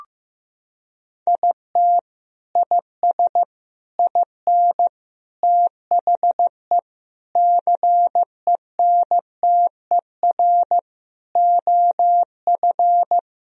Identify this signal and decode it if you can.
Morse code, heard as a keyed tone, about 15 wpm, 700 Hz. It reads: IT IS IN THE CENTER OF
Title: Sentences from Top 1000 Words - Review Part 2 - 15wpm